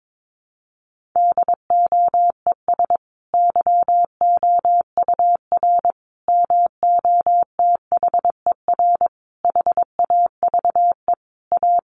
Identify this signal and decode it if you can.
Morse code, heard as a keyed tone, about 22 wpm, 700 Hz. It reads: DOES YOUR MOTHER HAVE A
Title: Sentences from Top 1400 Words 22wpm